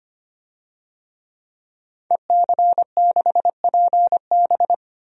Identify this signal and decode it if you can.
Morse code, heard as a keyed tone, about 25 wpm, 700 Hz. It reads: EC6PB
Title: DXCC-100 Call Signs - 5 Characters - 25wpm